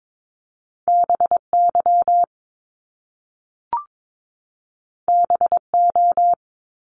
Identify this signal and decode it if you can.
Morse code, heard as a keyed tone, about 22 wpm, 700 Hz. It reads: BY BO
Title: Top Two Letter Combinations in Words 22wpm